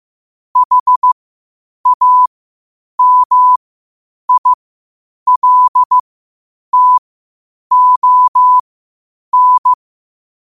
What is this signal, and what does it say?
Morse code, heard as a keyed tone, about 15 wpm, spaced as for 8 wpm, 1000 Hz. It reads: HAMILTON
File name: qotdfiles / 8wpm